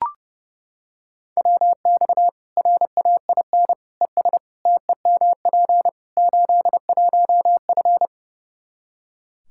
Morse code, heard as a keyed tone, about 32 wpm, 700 Hz. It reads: WX RAIN ES TEMP 81F
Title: Practice QSO - Exchange 2 - Info - 30wpm